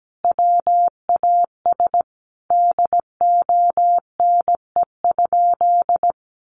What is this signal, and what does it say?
Morse code, heard as a keyed tone, about 17 wpm, 700 Hz. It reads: WAS DONE?
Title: Sentences from Top 1100 Words 17wpm